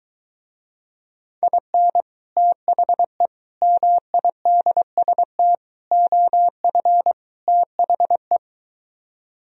Morse code, heard as a keyed tone, about 22 wpm, 700 Hz. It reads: IN THE MIDST OF THE